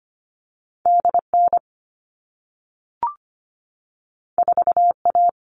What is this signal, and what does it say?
Morse code, heard as a keyed tone, about 25 wpm, 700 Hz. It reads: D N 4 A
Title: All Cut Numbers - 2-Digit 25wpm